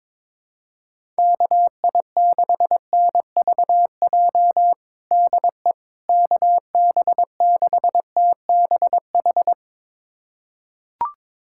Morse code, heard as a keyed tone, about 22 wpm, 700 Hz. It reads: KI6NVJ DE KB6TBH
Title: USA Callsign DE Callsign - 6 Characters - 22wpm